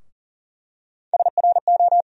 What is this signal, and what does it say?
Morse code, heard as a keyed tone, about 40 wpm, 700 Hz. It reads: SRO
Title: USA Call Sign Postfixes - 3 Letters 40wpm